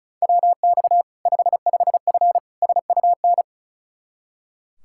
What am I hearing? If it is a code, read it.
Morse code, heard as a keyed tone, about 36 wpm, 700 Hz. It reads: WX 55F SUN
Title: K1USN SST QSO Practice 35wpm